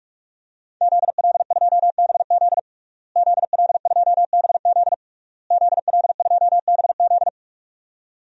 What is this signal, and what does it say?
Morse code, heard as a keyed tone, about 46 wpm, 700 Hz. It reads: ZL2BZ ZL2BZ ZL2BZ
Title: DXCC-100 Call Signs - 5 Characters - Repeated 3x - 45wpm